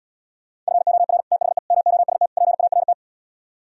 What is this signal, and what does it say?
Morse code, heard as a keyed tone, about 77 wpm, 700 Hz. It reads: FOR THE MOST PART